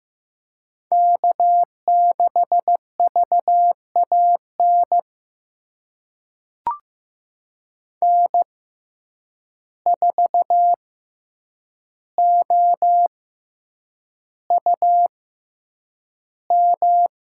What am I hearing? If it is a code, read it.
Morse code, heard as a keyed tone, about 15 wpm, spaced as for 8 wpm, 700 Hz. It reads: K6VAN N4OUM
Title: USA Call Signs - 5 Characters - 6x Character Spacing - 15wpm